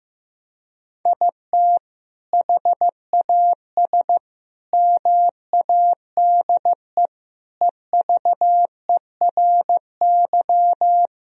Morse code, heard as a keyed tone, about 15 wpm, 700 Hz. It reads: IT HAS MADE EVERY
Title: Sentences from Top 1000 Words - Review Part 3 - 15wpm